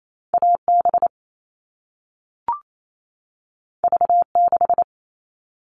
Morse code, heard as a keyed tone, about 28 wpm, 700 Hz. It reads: AB V6
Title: All Cut Numbers - 2-Digit 28wpm